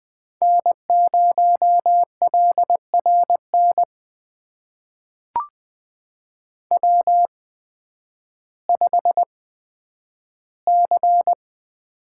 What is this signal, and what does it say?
Morse code, heard as a keyed tone, about 20 wpm, spaced as for 11 wpm, 700 Hz. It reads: N0LRN W5C
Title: USA Call Signs - 8x Character Spacing - 20wpm